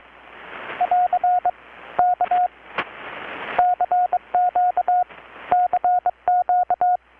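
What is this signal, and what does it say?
Morse code, heard as a keyed tone, about 22 wpm, 720 Hz, with radio noise. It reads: <AR> K CQ CQ
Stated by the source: Morse from KPH's Night of Nights XX maritime radiotelegraphy event, received in Chiapas, Mexico about 3,800 km away on a Kenwood R-600 with a grounded 10 m long-wire antenna.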